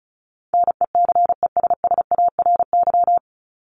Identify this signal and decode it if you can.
Morse code, heard as a keyed tone, about 34 wpm, 700 Hz. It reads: NECESSARY
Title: LY Suffix - Top 100 Words with Base Words 35wpm